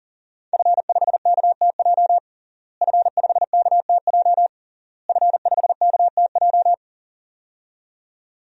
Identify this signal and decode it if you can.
Morse code, heard as a keyed tone, about 40 wpm, 700 Hz. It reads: F5KTJ F5KTJ F5KTJ
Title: DXCC-100 Call Signs - 5 Characters - Repeated 3x - 40wpm